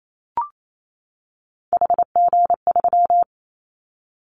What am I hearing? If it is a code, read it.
Morse code, heard as a keyed tone, about 28 wpm, 700 Hz. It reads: HG3
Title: Callsign Prefixes for Top 25 DXCC - 28wpm